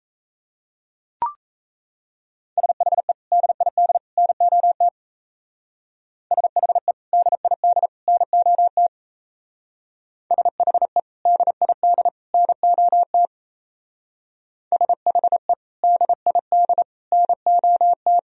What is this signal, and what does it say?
Morse code, heard as a keyed tone, about 34 wpm, 700 Hz. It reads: SHE DID NOT SHE DID NOT SHE DID NOT SHE DID NOT
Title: Speed-Racing - Sets of 3 Words 28wpm